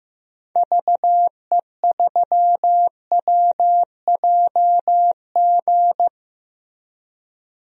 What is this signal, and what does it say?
Morse code, heard as a keyed tone, about 15 wpm, 700 Hz. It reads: VE3WJG
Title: DXCC-100 Call Signs - 6 Characters - Repeated 2x - 15wpm